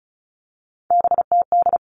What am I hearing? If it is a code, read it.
Morse code, heard as a keyed tone, about 34 wpm, 700 Hz. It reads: BTD
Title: All Cut Numbers - 3-Digit 35wpm